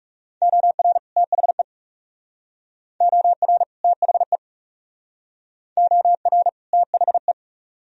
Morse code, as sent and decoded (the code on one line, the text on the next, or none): --- .-. / - .... / --- .-. / - .... / --- .-. / - .... .
OR THE OR THE OR THE